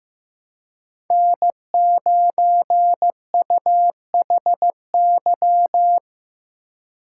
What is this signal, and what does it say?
Morse code, heard as a keyed tone, about 16 wpm, 700 Hz. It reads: N9UHY
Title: USA Call Signs - 5 Characters - 3x Character Spacing - 15wpm